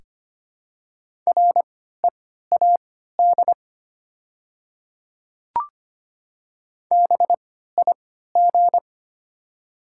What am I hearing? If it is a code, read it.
Morse code, heard as a keyed tone, about 25 wpm, spaced as for 14 wpm, 700 Hz. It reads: READ BIG
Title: 3x Character Spacing - Top 300 Words - 25wpm